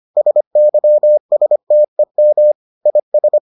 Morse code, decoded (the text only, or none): SYSTEM IS